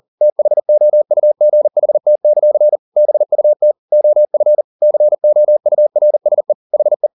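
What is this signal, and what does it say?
Morse code, heard as a keyed tone, about 40 wpm, 600 Hz. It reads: THOUGHT; BUT OF COURSE HE
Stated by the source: Morse code at 40 wpm, practice text, CW med Gnister.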